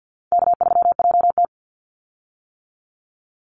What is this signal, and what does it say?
Morse code, heard as a keyed tone, about 50 wpm, 700 Hz. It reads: K3JT
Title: Easy Call Signs - 50wpm